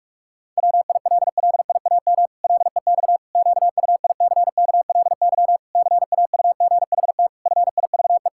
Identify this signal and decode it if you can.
Morse code, heard as a keyed tone, about 46 wpm, 700 Hz. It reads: WILLIAM LEX QUICKLY CAUGHT FIVE
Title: Pangram Challenge v2 - 45wpm